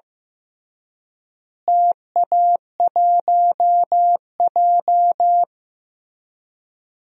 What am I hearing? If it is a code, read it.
Morse code, heard as a keyed tone, about 15 wpm, 700 Hz. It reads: TA1J
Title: DXCC-100 Call Signs - 4 Characters - Repeated 3x - 15wpm